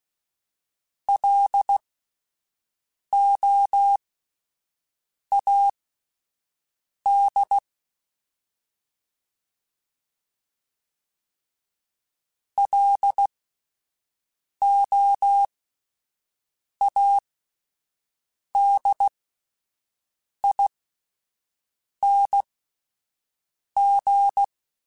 Morse code, heard as a keyed tone, about 16 wpm, spaced as for 5 wpm, 800 Hz. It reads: LOAD LOADING
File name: Alphabetical / Alphabetical mp3s / Ls @5